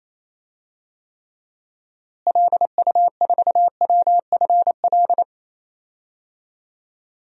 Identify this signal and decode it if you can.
Morse code, heard as a keyed tone, about 27 wpm, 700 Hz. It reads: LU4WFL
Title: DXCC-100 Call Signs - 6 Characters - Repeated 2x - 28wpm